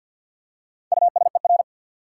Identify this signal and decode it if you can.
Morse code, heard as a keyed tone, about 53 wpm, 700 Hz. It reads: USER
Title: Four Letter Words - 50wpm